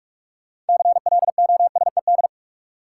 Morse code, heard as a keyed tone, about 46 wpm, 700 Hz. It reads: CLOSED
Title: Sentences from Top 1400 Words - New Word Drill - 45wpm